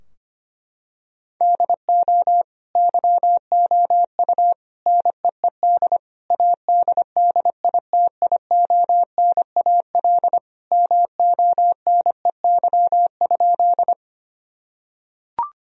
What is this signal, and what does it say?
Morse code, heard as a keyed tone, about 25 wpm, 700 Hz. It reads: DO YOU NEED ADDITIONAL MONEY?
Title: Sentences from Top 1000 Words - Review Part 4 - 25wpm